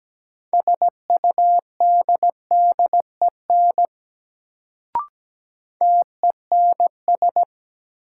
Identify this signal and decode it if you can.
Morse code, heard as a keyed tone, about 17 wpm, 700 Hz. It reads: SUDDEN TENS
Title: Sets of 4 Words - New Word Drill - 17wpm